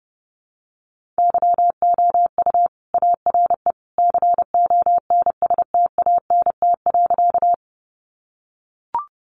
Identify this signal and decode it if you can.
Morse code, heard as a keyed tone, about 30 wpm, 700 Hz. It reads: YOU ARE CONSTANT.
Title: Sentences from Top 1500 Words 30wpm